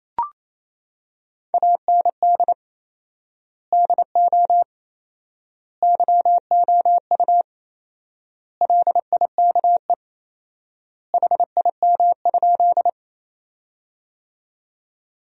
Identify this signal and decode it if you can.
Morse code, heard as a keyed tone, about 28 wpm, 700 Hz. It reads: AND DO YOU LIKE HIM?
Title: 4x Word-Spacing - Sentences from Top 100 Words - 28wpm